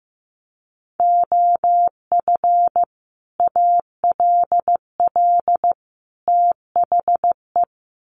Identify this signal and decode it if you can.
Morse code, heard as a keyed tone, about 15 wpm, 700 Hz. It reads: OF ALL THE
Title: Sets of 3 Words 15wpm